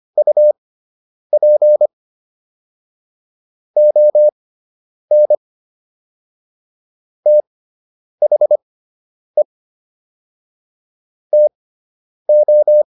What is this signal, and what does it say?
Morse code, heard as a keyed tone, about 25 wpm, spaced as for 9 wpm, 600 Hz. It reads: UP ON THE TO